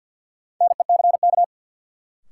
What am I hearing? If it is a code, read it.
Morse code, heard as a keyed tone, about 51 wpm, 700 Hz. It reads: NECK
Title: Sets of 5 Words Encore - New Word Drill - 50wpm